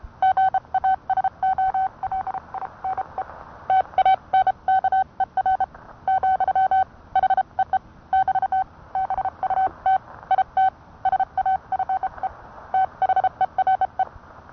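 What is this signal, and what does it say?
Morse code, heard as a keyed tone, about 30 wpm, 760 Hz. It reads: GASOLINE TANKER , HEE <BT> BUT IT SAFE THERE
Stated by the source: Morse code sent by N1EA on a Vibroplex key.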